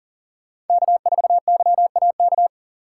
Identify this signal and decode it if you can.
Morse code, heard as a keyed tone, about 40 wpm, 700 Hz. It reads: K4YAK
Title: Speed Racing - USA 1x3 Call Signs 40wpm